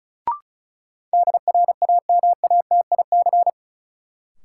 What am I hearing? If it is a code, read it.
Morse code, heard as a keyed tone, about 36 wpm, 700 Hz. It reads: DRAMATIC
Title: Sentences from Top 1300 Words - New Word Drill Alt - 35wpm